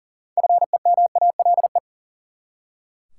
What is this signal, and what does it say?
Morse code, heard as a keyed tone, about 40 wpm, 700 Hz. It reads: FEMALE
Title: Six Letter Words - 40wpm